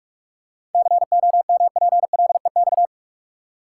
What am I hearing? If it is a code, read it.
Morse code, heard as a keyed tone, about 46 wpm, 700 Hz. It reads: COMPLEX
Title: Seven Letter Words - 45wpm